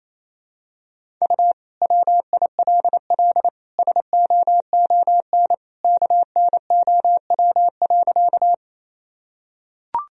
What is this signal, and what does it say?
Morse code, heard as a keyed tone, about 28 wpm, 700 Hz. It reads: U WILL SOON KNOW.
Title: Sentences from Top 400 Words and Abbreviations 28wpm